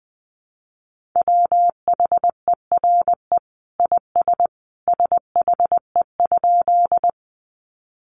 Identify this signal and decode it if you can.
Morse code, heard as a keyed tone, about 20 wpm, 700 Hz. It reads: WHERE IS SHE?
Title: Sentences from Top 200 Words 20wpm